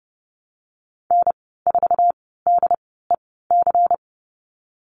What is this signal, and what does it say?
Morse code, heard as a keyed tone, about 30 wpm, spaced as for 17 wpm, 700 Hz. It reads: N4DEC
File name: Lesson 37g - Letter x - Call Signs 30wpm